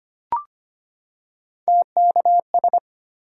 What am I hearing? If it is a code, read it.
Morse code, heard as a keyed tone, about 25 wpm, 700 Hz. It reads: TKS
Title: Top 90 QSO Elements 25wpm